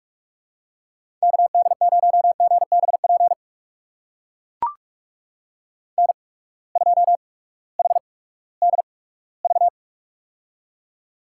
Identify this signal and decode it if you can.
Morse code, heard as a keyed tone, about 42 wpm, spaced as for 20 wpm, 700 Hz. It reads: KD0GBP N2HDV
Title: USA Call Signs - 8x Character Spacing - 45wpm